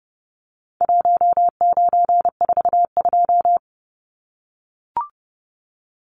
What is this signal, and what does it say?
Morse code, heard as a keyed tone, about 30 wpm, 700 Hz. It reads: 1942
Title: Numbers - Contest Style - 4-Digit 30wpm